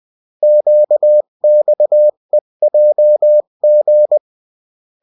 Morse code, heard as a keyed tone, about 20 wpm, 600 Hz. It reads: QXEJG
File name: morse mp3 / revision9-vitesse 18 mots